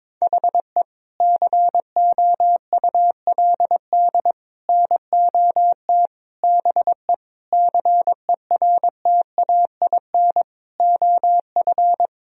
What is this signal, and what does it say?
Morse code, heard as a keyed tone, about 22 wpm, 700 Hz. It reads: HE COULD NOT BE CERTAIN OF
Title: Sentences from Top 1000 Words - Review Part 2 - 22wpm